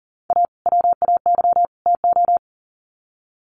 Morse code, heard as a keyed tone, about 40 wpm, 700 Hz. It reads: A WAY TO
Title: Sets of 3 Words 40wpm